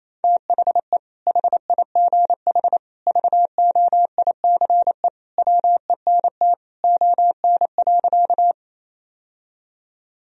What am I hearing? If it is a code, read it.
Morse code, heard as a keyed tone, about 28 wpm, 700 Hz. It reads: THE HIGH VOICE WENT ON.
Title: Sentences from Top 1000 Words - Review Part 1 - 28wpm